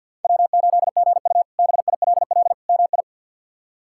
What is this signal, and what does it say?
Morse code, heard as a keyed tone, about 47 wpm, 700 Hz. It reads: W8GU BILL MI
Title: K1USN SST QSO Practice - 50wpm